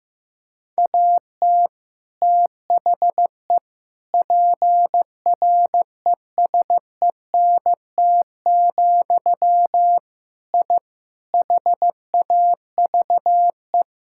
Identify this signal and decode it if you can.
Morse code, heard as a keyed tone, about 15 wpm, 700 Hz. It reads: AT THE PRESENT, I HAVE
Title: Sentences from Top 1000 Words - Review Part 5 - 15wpm